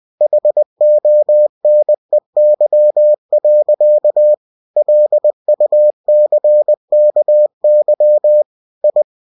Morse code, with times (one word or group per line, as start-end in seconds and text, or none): 0.2-4.3 HONEY.
4.8-8.4 LUCKY
8.8-9.0 I